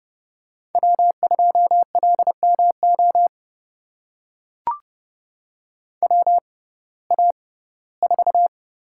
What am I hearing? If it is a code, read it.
Morse code, heard as a keyed tone, about 31 wpm, spaced as for 19 wpm, 700 Hz. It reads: W2LMO WA4
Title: USA Call Signs - 5 Characters - 6x Character Spacing - 30wpm